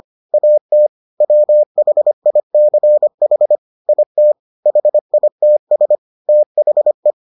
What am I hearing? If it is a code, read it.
Morse code, heard as a keyed tone, about 25 wpm, 600 Hz. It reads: AT WHICH IT HITS THE